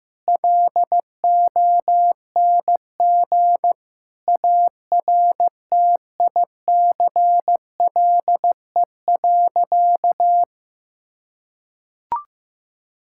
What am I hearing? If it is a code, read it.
Morse code, heard as a keyed tone, about 16 wpm, 700 Hz. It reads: LONG ARTICLE.
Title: Sentences from Top 1000 Words - Review Part 5 - 15wpm